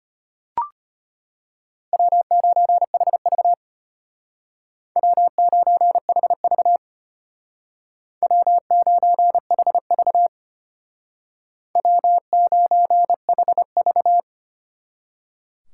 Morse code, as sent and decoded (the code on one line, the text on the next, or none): .-- ----. .... ...- / .-- ----. .... ...- / .-- ----. .... ...- / .-- ----. .... ...-
W9HV W9HV W9HV W9HV